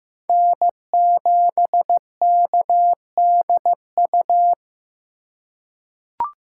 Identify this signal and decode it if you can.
Morse code, heard as a keyed tone, about 15 wpm, 700 Hz. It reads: N7KDU
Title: USA Call Signs - 5 Characters - 3x Character Spacing - 15wpm